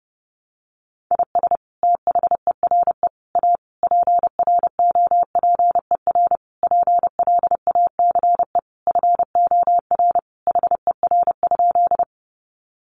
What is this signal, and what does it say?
Morse code, heard as a keyed tone, about 30 wpm, 700 Hz. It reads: IS THERE A PROPER PLACE FOR HER?
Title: Sentences from Top 800 Words 30wpm